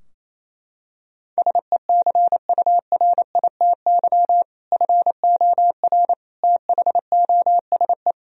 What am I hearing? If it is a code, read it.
Morse code, heard as a keyed tone, about 28 wpm, 700 Hz. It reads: SECURITY FOR THOSE
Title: Sentences from Top 1000 Words - Review Part 4 - 28wpm